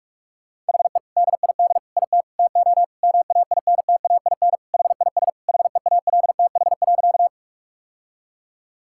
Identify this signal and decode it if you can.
Morse code, heard as a keyed tone, about 42 wpm, 700 Hz. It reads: HE DID IT TO MAINTAIN HIS HEALTH.
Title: Sentences from Top 1000 Words - Review Part 5 - 45wpm